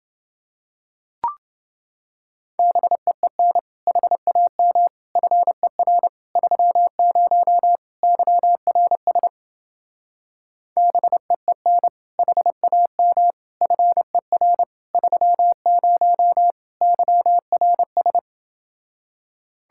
Morse code, heard as a keyed tone, about 29 wpm, 700 Hz. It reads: BEEN HAM FER 30 YRS BEEN HAM FER 30 YRS